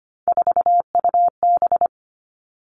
Morse code, heard as a keyed tone, about 25 wpm, 700 Hz. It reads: 4UB
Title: All Cut Numbers - 3-Digit 25wpm